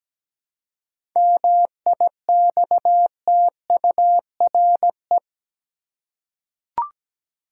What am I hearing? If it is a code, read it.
Morse code, heard as a keyed tone, about 17 wpm, 700 Hz. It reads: MIXTURE
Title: Seven Letter Words - 17wpm